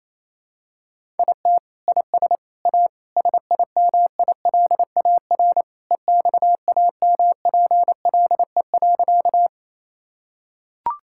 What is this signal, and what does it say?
Morse code, heard as a keyed tone, about 28 wpm, 700 Hz. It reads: IT IS A SIMILAR EXAMPLE.
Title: Sentences from Top 1000 Words - Review Part 5 - 28wpm